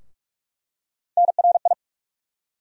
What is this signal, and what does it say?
Morse code, heard as a keyed tone, about 46 wpm, 700 Hz. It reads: NRI
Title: USA Call Sign Postfixes - 3 Letters 45wpm